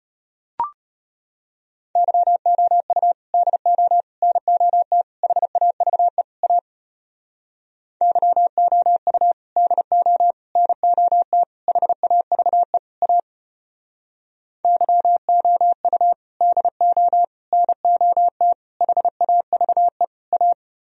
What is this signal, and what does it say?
Morse code, heard as a keyed tone, about 32 wpm, 700 Hz. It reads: YOU DO NOT HAVE A YOU DO NOT HAVE A YOU DO NOT HAVE A